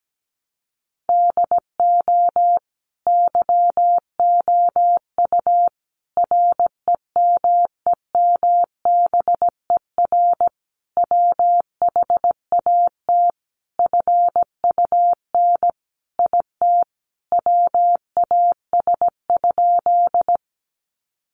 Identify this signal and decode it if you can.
Morse code, heard as a keyed tone, about 17 wpm, 700 Hz. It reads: DO YOU REMEMBER WHAT FUN IT WAS?
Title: Sentences from Top 700 Words 17wpm